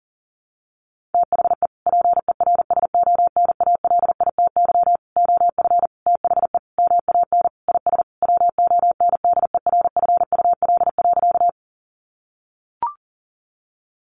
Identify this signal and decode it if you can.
Morse code, heard as a keyed tone, about 37 wpm, 700 Hz. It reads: THE PERSONALITY OF THE MAN IS WONDERFUL.
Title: Sentences from Top 1400 Words 40wpm